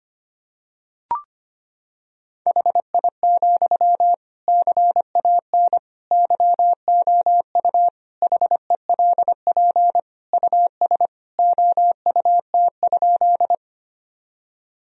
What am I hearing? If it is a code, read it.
Morse code, heard as a keyed tone, about 25 wpm, 700 Hz. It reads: HI, CAN YOU HELP US OUT?
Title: Sentences from Top 1000 Words - Review Part 3 - 25wpm